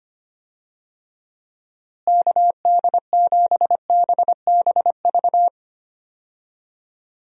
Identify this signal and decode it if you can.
Morse code, heard as a keyed tone, about 25 wpm, 700 Hz. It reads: KD7BBV
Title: USA Call Signs - 6 Characters - Repeated 2x - 25wpm